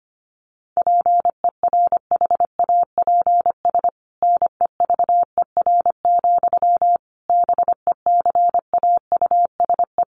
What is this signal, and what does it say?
Morse code, heard as a keyed tone, about 25 wpm, 700 Hz. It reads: PERHAPS NEVER, BECAUSE
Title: Sentences from Top 1200 Words 25wpm